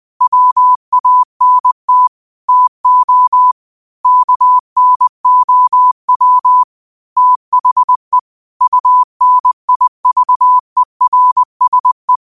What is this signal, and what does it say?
Morse code, heard as a keyed tone, about 20 wpm, 1000 Hz. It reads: WANT TO KNOW THE UNIVERSE
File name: qotdfiles / 20wpm